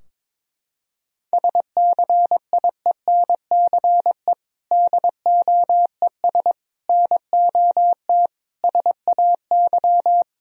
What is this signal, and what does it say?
Morse code, heard as a keyed tone, about 22 wpm, 700 Hz. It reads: SCIENCE DOES NOT SAY
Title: Sentences from Top 1000 Words - Review Part 2 - 22wpm